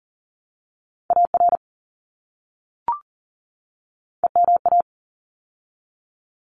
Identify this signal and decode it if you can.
Morse code, heard as a keyed tone, about 40 wpm, 700 Hz. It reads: AR EMA
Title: ARRL Sections 40wpm